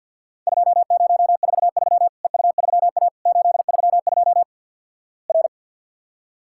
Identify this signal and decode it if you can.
Morse code, heard as a keyed tone, about 48 wpm, 700 Hz. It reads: 2043 EU3A 832 R
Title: Worked All Europe DX Contest - QTC Exchange - 50wpm